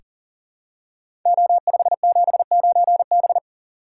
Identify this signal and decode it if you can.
Morse code, heard as a keyed tone, about 39 wpm, 700 Hz. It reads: O579B
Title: Code Groups - 5 Letter:Numbers - 40wpm